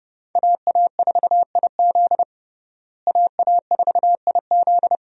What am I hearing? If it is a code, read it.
Morse code, heard as a keyed tone, about 32 wpm, 700 Hz. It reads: AA4IZ AA4IZ
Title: USA Call Signs - 5 Characters - Repeated 3x - 30wpm